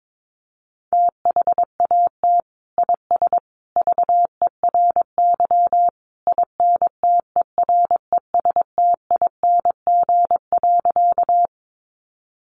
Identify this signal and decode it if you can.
Morse code, heard as a keyed tone, about 22 wpm, 700 Hz. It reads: THAT IS VERY INTERESTING.
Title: Sentences from Top 1000 Words 22wpm